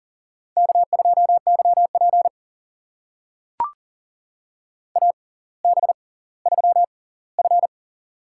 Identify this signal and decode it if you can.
Morse code, heard as a keyed tone, about 40 wpm, spaced as for 20 wpm, 700 Hz. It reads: K2YP AB3F
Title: USA Call Signs - 4 Characters - 6x Character Spacing - 40wpm